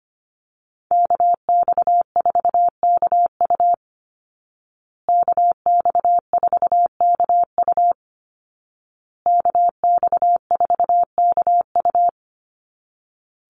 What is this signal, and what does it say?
Morse code, heard as a keyed tone, about 25 wpm, 700 Hz. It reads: KX4KU KX4KU KX4KU
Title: Call Signs Repeated 2x - 25wpm